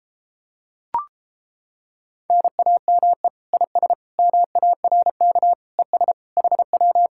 Morse code, heard as a keyed tone, about 29 wpm, 700 Hz. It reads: NAME IS MARK ES HW